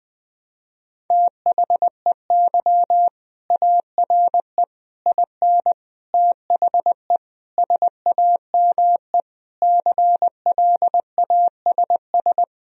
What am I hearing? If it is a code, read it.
Morse code, heard as a keyed tone, about 21 wpm, 700 Hz. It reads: THEY ARE IN THE SAME CLASS
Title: Sentences from Top 1000 Words - Review Part 2 - 20wpm